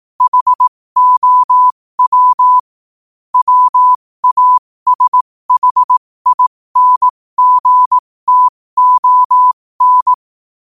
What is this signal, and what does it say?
Morse code, heard as a keyed tone, about 18 wpm, spaced as for 15 wpm, 1000 Hz. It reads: HOW WASHINGTON
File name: qotdfiles / 15wpm